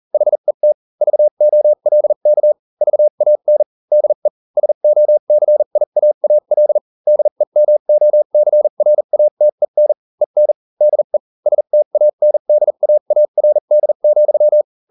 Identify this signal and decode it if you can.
Morse code, heard as a keyed tone, about 40 wpm, spaced as for 35 wpm, 600 Hz. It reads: HET VOLK VAN DE SOCIAAL DEMOCRATEN EN DE STANDAARD,